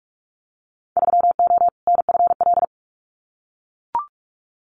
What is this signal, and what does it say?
Morse code, heard as a keyed tone, about 44 wpm, 700 Hz. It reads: 3O NFL
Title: Winter Field Day Exchanges 45wpm